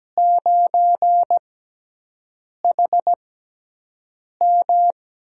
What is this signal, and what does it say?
Morse code, heard as a keyed tone, about 17 wpm, spaced as for 6 wpm, 700 Hz. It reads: 9HM
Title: USA Call Signs - 4 Characters - 6x Character Spacing - 17wpm